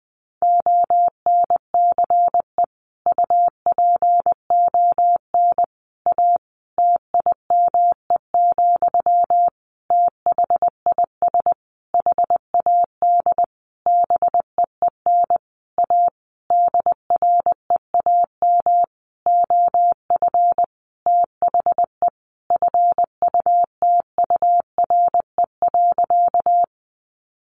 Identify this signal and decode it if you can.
Morse code, heard as a keyed tone, about 20 wpm, 700 Hz. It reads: ONCE UPON A TIME, THIS HAD BEEN A DREAM OF THE FUTURE.
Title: Sentences from Top 1500 Words 20wpm